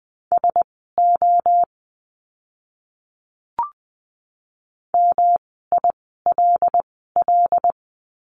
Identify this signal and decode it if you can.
Morse code, heard as a keyed tone, about 20 wpm, spaced as for 15 wpm, 700 Hz. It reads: SO MILL
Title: 108 Lesson 18f - Letter m - Words 20wpm